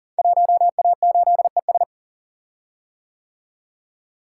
Morse code, heard as a keyed tone, about 39 wpm, 700 Hz. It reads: 1A8ES
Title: Code Groups - 5 Letter:Numbers - 40wpm